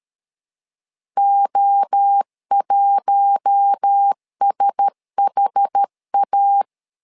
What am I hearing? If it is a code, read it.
Morse code, heard as a keyed tone, about 13 wpm, 800 Hz. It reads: O1SHA
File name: LEZIONI / 4 Q1S 12wpm 1802